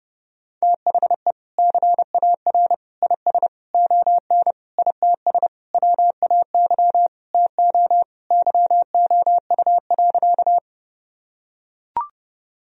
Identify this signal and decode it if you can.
Morse code, heard as a keyed tone, about 30 wpm, 700 Hz. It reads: THE CAR IS ON ITS WAY TO YOU.
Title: Sentences from Top 1000 Words - Review Part 1 - 30wpm